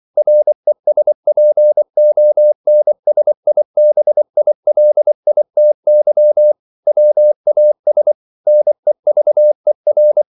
Morse code, heard as a keyed tone, about 24 wpm, 600 Hz. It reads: RESPONSIBILITY WAS NEVER